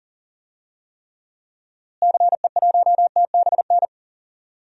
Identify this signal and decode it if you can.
Morse code, heard as a keyed tone, about 40 wpm, 700 Hz. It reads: CE1TBN
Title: DXCC-100 Call Signs - 6 Characters - Repeated 3x - 40wpm